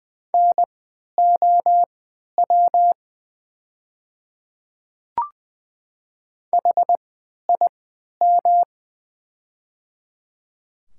Morse code, heard as a keyed tone, about 20 wpm, spaced as for 11 wpm, 700 Hz. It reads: NOW HIM
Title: 3x Character Spacing - Top 100 Words - 20wpm